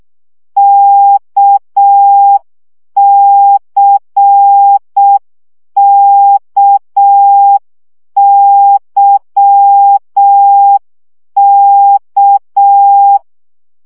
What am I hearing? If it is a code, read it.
Morse code, heard as a keyed tone, about 6 wpm, 800 Hz. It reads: KCKYK
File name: CW gyakorlas k c y 5 baud